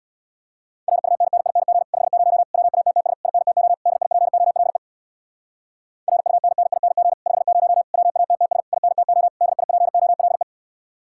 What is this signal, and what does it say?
Morse code, heard as a keyed tone, about 64 wpm, 700 Hz. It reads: RUNNING 50 WATTS INTO DIPOLE RUNNING 50 WATTS INTO DIPOLE